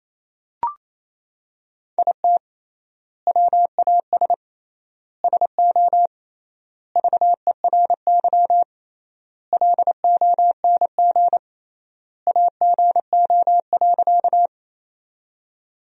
Morse code, heard as a keyed tone, about 28 wpm, 700 Hz. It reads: IT WAS SO VERY LONG AGO.
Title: 3x Word-Spacing - Sentences from Top 500 Words - 28wpm